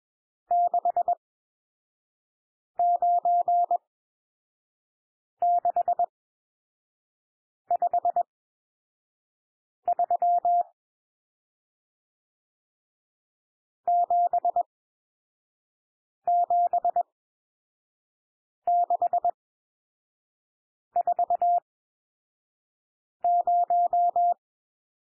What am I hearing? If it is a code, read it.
Morse code, heard as a keyed tone, about 21 wpm, spaced as for 5 wpm, 700 Hz. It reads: 69653 77640
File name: cwmp3 / N1 SOUND NUMBERS 21 wpm